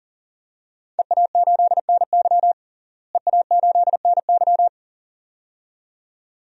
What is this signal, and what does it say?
Morse code, heard as a keyed tone, about 40 wpm, 700 Hz. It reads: EA8NY EA8NY
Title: DXCC-100 Call Signs - 5 Characters - Repeated 2x - 40wpm